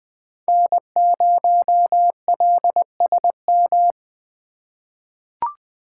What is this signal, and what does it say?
Morse code, heard as a keyed tone, about 20 wpm, 700 Hz. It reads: N0LSM
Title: USA Call Signs - 5 Characters - 2x Character Spacing - 20wpm